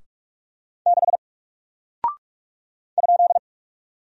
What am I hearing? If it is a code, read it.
Morse code, heard as a keyed tone, about 44 wpm, 700 Hz. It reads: 6 ?
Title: Single Character (Letter:Number:Punctuation:Prosign) Alt 45wpm